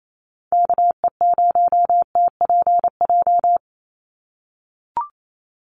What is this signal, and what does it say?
Morse code, heard as a keyed tone, about 28 wpm, 700 Hz. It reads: KE0TPJ
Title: USA Call Signs 28wpm